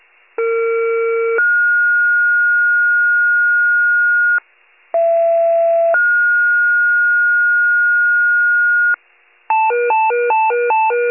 Two-tone page, heard as a one-second tone, then a three-second tone, about 460 then 1430 Hz, about 670 then 1430 Hz. It is heard from 0.4 to 8.9 s.
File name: pager / 21 FIRE / wcalert